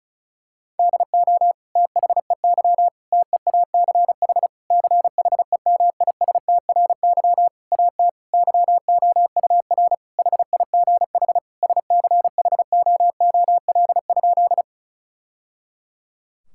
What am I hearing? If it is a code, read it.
Morse code, heard as a keyed tone, about 36 wpm, 700 Hz. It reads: DO THEY TEACH CHEMISTRY AT YOUR HIGH SCHOOL?
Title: Sentences from Top 1000 Words - Review Part 6 - 35wpm